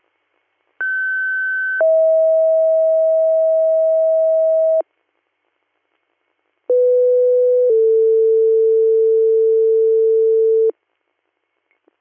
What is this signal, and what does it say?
Two-tone page, heard as a one-second tone, then a three-second tone, about 1550 then 640 Hz, about 500 then 450 Hz. It is heard from 0.8 to 10.7 s.